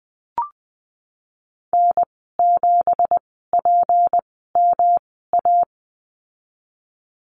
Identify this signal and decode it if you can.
Morse code, heard as a keyed tone, about 20 wpm, spaced as for 15 wpm, 700 Hz. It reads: N7PMA
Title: 186 Lesson 30g - Letter b - Call Signs 20wpm